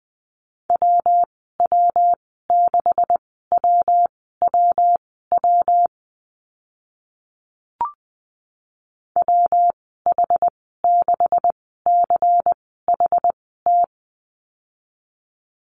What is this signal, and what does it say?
Morse code, heard as a keyed tone, about 20 wpm, spaced as for 15 wpm, 700 Hz. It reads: WW6WWW WH6CHT
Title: 140 Lesson 23g - Letter f - Call Signs 20wpm